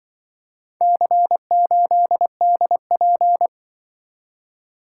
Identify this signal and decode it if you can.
Morse code, heard as a keyed tone, about 24 wpm, 700 Hz. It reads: C8DP
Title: Speed-Racing - Four Letter-Number Combinations - 20wpm